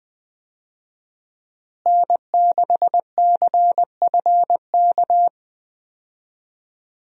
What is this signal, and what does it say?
Morse code, heard as a keyed tone, about 20 wpm, 700 Hz. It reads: N6CFK
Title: USA Call Signs - 5 Characters - Repeated 3x - 20wpm